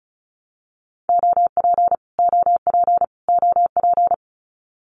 Morse code, heard as a keyed tone, about 34 wpm, 700 Hz. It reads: OP OP OP
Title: Abbreviations Repeated 3x - 35wpm